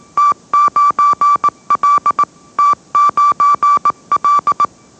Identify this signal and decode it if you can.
Morse code, heard as a keyed tone, about 18 wpm, 1160 Hz. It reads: T9L T9L
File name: mrx / mrx081 20wpm